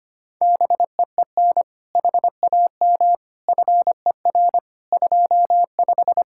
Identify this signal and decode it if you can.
Morse code, heard as a keyed tone, about 24 wpm, 700 Hz. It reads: BEEN HAM FER 25
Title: Speed-Racing Common QSO Phrases Encore - 25wpm